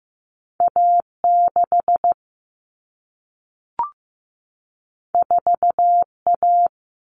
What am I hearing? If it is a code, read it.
Morse code, heard as a keyed tone, about 15 wpm, 700 Hz. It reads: A6 4A
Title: All Cut Numbers - 2-Digit 15wpm